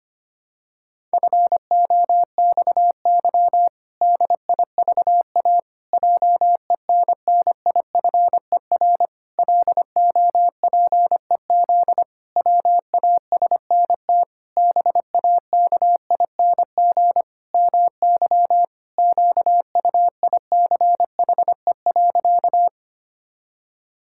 Morse code, heard as a keyed tone, about 25 wpm, 700 Hz. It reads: FOXY DIVA JENNIFER LOPEZ WASNT BAKING MY QUICHE.
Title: Pangram Challenge v2 - 25wpm